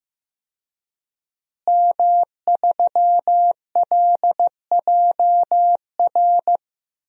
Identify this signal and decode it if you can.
Morse code, heard as a keyed tone, about 15 wpm, 700 Hz. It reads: M3LJR
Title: DXCC-100 Call Signs - 5 Characters - Repeated 3x - 15wpm